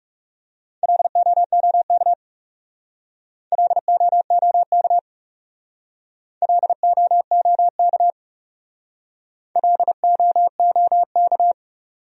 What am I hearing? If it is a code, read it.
Morse code, heard as a keyed tone, about 38 wpm, 700 Hz. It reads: LOOK LOOK LOOK LOOK
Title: Speed-Racing Top 100 Words 30wpm